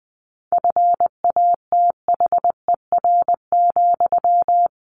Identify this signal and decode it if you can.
Morse code, heard as a keyed tone, about 20 wpm, 700 Hz. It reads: FATHER,
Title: Sentences from Top 1200 Words 20wpm